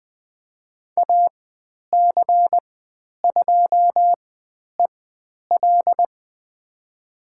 Morse code, heard as a keyed tone, about 20 wpm, spaced as for 10 wpm, 700 Hz. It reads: AC2EL
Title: 095 Lesson 16g - Letter c - Call Signs 20wpm